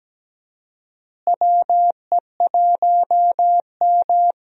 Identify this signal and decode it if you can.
Morse code, heard as a keyed tone, about 17 wpm, 700 Hz. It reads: WE1M
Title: USA Call Signs - 4 Characters - Repeated 3x - 17wpm